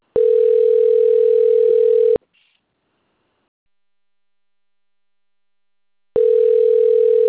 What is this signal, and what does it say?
Ringback tone (North America), 440 + 480 Hz.
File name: out- - - - - .38701